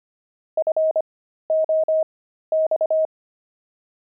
Morse code, heard as a keyed tone, about 25 wpm, spaced as for 13 wpm, 630 Hz. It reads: FOX
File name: pangram-fs-13-cs-25-jay